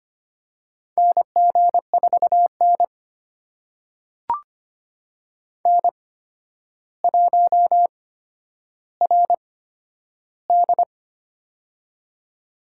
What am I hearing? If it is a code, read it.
Morse code, heard as a keyed tone, about 25 wpm, spaced as for 11 wpm, 700 Hz. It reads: NG4N N1RD
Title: USA Call Signs - 4 Characters - 8x Character Spacing - 25wpm